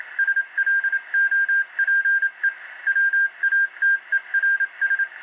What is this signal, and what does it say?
Morse code, heard as a keyed tone, about 27 wpm, 1600 Hz. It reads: SLOPE WATERS